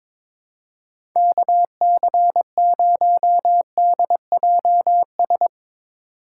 Morse code, heard as a keyed tone, about 22 wpm, 700 Hz. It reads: KC0DJS
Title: USA Call Signs - 6 Characters - 3x Character Spacing - 22wpm